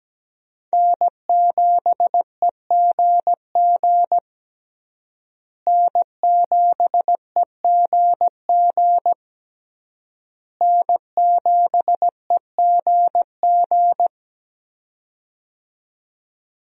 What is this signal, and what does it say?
Morse code, heard as a keyed tone, about 17 wpm, 700 Hz. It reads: N7EGG N7EGG N7EGG
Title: USA Call Signs - 5 Characters - Repeated 3x - 17wpm